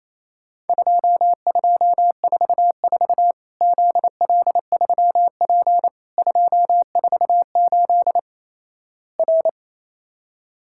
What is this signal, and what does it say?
Morse code, heard as a keyed tone, about 28 wpm, 700 Hz. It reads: 2244 ZL3P 248 R
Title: Worked All Europe DX Contest - QTC Exchange - 28wpm